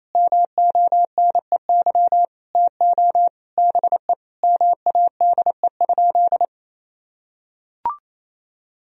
Morse code, heard as a keyed tone, about 28 wpm, 700 Hz. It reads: MONEY TO BE MADE?
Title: Sentences from Top 1000 Words - Review Part 4 - 28wpm